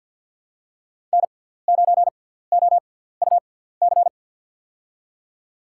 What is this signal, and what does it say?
Morse code, heard as a keyed tone, about 49 wpm, 700 Hz. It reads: N 9 O U C
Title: USA Call Signs - 5 Characters - 6x Character Spacing - 50wpm